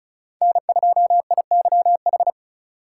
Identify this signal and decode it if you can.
Morse code, heard as a keyed tone, about 36 wpm, 700 Hz. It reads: N2IYH
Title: USA Call Signs - 5 Characters - 3x Character Spacing - 35wpm